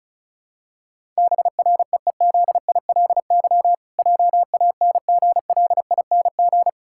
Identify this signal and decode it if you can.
Morse code, heard as a keyed tone, about 36 wpm, 700 Hz. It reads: BREEZILY JANGLING
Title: Pangram Challenge v2 - 35wpm